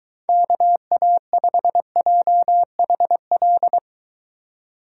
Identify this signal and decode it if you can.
Morse code, heard as a keyed tone, about 23 wpm, 700 Hz. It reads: KA5JHL